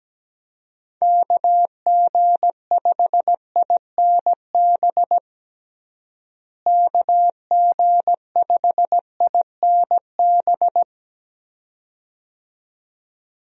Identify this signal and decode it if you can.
Morse code, heard as a keyed tone, about 17 wpm, 700 Hz. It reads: KG5INB KG5INB
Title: USA Call Signs - 6 Characters - Repeated 2x - 17wpm